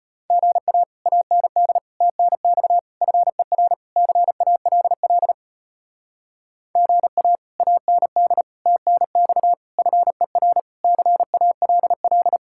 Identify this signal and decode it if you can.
Morse code, heard as a keyed tone, about 33 wpm, 700 Hz. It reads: GA AND TNX FER CALL GA AND TNX FER CALL